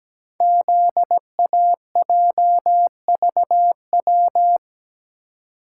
Morse code, heard as a keyed tone, about 17 wpm, 700 Hz. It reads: ZAJVW
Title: Code Groups - 5 Letters - 17wpm